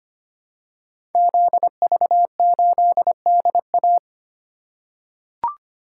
Morse code, heard as a keyed tone, about 25 wpm, 700 Hz. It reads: ZV8DA
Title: Code Groups - 5 Letter:Numbers - 25wpm